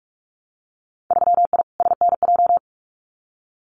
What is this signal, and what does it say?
Morse code, heard as a keyed tone, about 46 wpm, 700 Hz. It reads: 3I SNJ
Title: Winter Field Day Exchanges Encore 45wpm